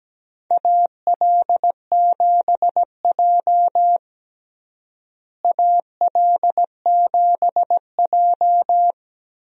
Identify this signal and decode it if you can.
Morse code, heard as a keyed tone, about 17 wpm, 700 Hz. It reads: AL7J AL7J
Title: USA Call Signs - 4 Characters - Repeated 2x - 17wpm